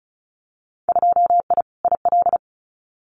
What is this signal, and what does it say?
Morse code, heard as a keyed tone, about 36 wpm, 700 Hz. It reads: 2I IL
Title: Winter Field Day Exchanges Encore 35wpm